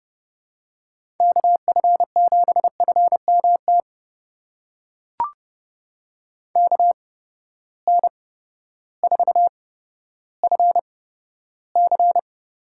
Morse code, heard as a keyed tone, about 32 wpm, spaced as for 14 wpm, 700 Hz. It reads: KF7FMT KN4FC
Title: USA Call Signs - 8x Character Spacing - 30wpm